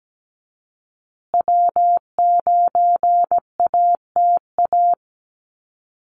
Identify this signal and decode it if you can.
Morse code, heard as a keyed tone, about 17 wpm, 700 Hz. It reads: W9ATA
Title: USA Call Signs Repeated 3x 17wpm